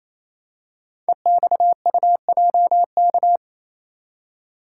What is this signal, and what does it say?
Morse code, heard as a keyed tone, about 27 wpm, 700 Hz. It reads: EXUJK
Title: Code Groups - 5 Letters - 28wpm